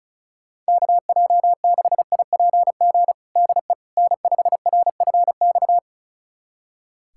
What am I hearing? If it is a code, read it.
Morse code, heard as a keyed tone, about 36 wpm, 700 Hz. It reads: KJ6IPG DE N5RFX
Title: USA Callsign DE Callsign 35wpm